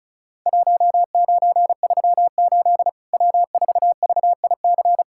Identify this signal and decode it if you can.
Morse code, heard as a keyed tone, about 36 wpm, 700 Hz. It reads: 1938 W4VIC
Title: Worked All Europe DX Contest - QTC Exchange - 35wpm